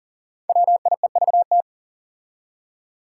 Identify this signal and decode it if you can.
Morse code, heard as a keyed tone, about 40 wpm, 700 Hz. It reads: WIEVT
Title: Five Challenging Letter-Number Combinations 40wpm